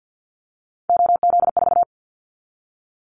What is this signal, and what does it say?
Morse code, heard as a keyed tone, about 50 wpm, 700 Hz. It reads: OZ4
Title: Callsign Prefixes for Top 50 DXCC - 50wpm